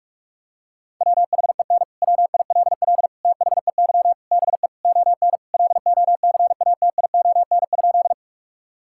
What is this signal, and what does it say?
Morse code, heard as a keyed tone, about 46 wpm, 700 Hz. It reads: WHEN WILL THEY BE ON LOCATION?
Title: Sentences from Top 1000 Words - Review Part 6 - 45wpm